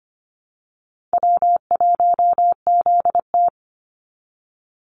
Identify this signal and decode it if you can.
Morse code, heard as a keyed tone, about 25 wpm, 700 Hz. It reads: W1ZT
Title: Easy Call Signs Repeated 2x - 25wpm